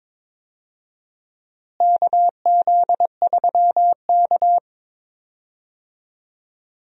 Morse code, heard as a keyed tone, about 22 wpm, 700 Hz. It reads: KZ3K
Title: USA Call Signs - 4 Characters - Repeated 3x - 22wpm